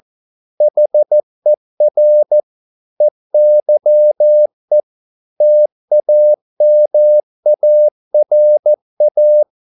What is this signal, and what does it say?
Morse code, heard as a keyed tone, about 14 wpm, 600 Hz. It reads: HER EYE TAMARA